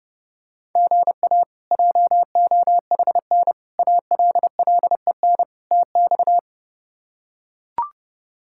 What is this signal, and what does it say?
Morse code, heard as a keyed tone, about 28 wpm, 700 Hz. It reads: GA JOHN ALLEN TX
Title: K1USN SST QSO Practice - 30wpm